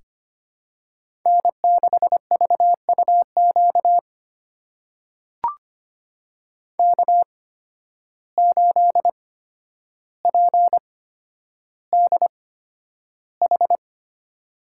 Morse code, heard as a keyed tone, about 25 wpm, spaced as for 11 wpm, 700 Hz. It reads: N6VUQ K8PDH
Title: USA Call Signs - 5 Characters - 8x Character Spacing - 25wpm